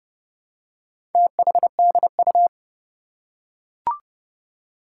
Morse code, heard as a keyed tone, about 30 wpm, 690 Hz. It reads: THDU
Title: Four Challenging Letter-Number Combinations 30wpm